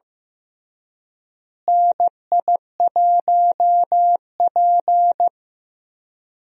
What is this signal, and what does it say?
Morse code, heard as a keyed tone, about 15 wpm, 700 Hz. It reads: NI1P
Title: USA Call Signs - 4 Characters - Repeated 3x - 15wpm